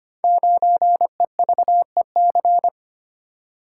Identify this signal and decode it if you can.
Morse code, heard as a keyed tone, about 25 wpm, 700 Hz. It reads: 9EVEC
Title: Code Groups - 5 Letter:Numbers - 25wpm